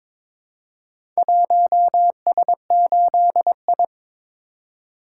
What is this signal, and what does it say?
Morse code, heard as a keyed tone, about 22 wpm, 700 Hz. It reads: 1S8I
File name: practice / 22wpm / Four Letter-Number Combinations - 22wpm